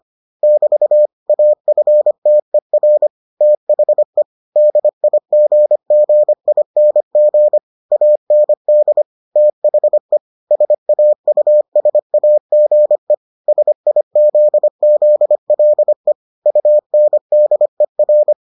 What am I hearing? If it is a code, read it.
Morse code, heard as a keyed tone, about 25 wpm, 600 Hz. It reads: <BT> AFTER THE DIGGING AND THE SAUSAGE SIZZLE UNDER